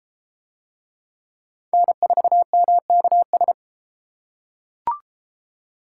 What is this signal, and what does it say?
Morse code, heard as a keyed tone, about 32 wpm, 700 Hz. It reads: N4MKS